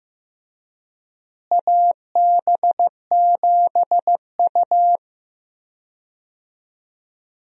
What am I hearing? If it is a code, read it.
Morse code, heard as a keyed tone, about 15 wpm, 700 Hz. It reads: AB7U
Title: USA Call Signs - 4 Characters - Repeated 3x - 15wpm